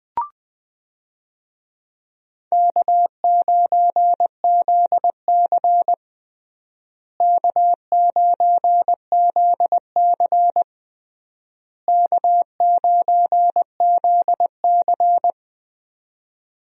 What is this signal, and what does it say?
Morse code, heard as a keyed tone, about 21 wpm, 690 Hz. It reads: K9ZC K9ZC K9ZC
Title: USA Call Signs - 4 Characters - Repeated 3x - 20wpm